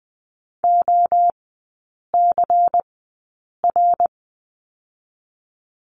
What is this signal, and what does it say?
Morse code, heard as a keyed tone, about 20 wpm, 700 Hz. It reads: O C R
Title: 114 Lesson 19e - Letter w - Three Characters 20wpm